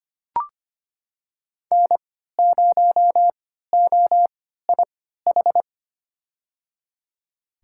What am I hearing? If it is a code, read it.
Morse code, heard as a keyed tone, about 25 wpm, spaced as for 14 wpm, 700 Hz. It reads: N0OIH
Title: USA Call Signs - 5 Characters - 3x Character Spacing - 25wpm